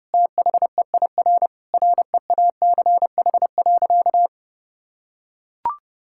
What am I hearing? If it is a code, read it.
Morse code, heard as a keyed tone, about 32 wpm, 700 Hz. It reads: THEIR REACH.
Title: Sentences from Top 1000 Words - Review Part 6 - 30wpm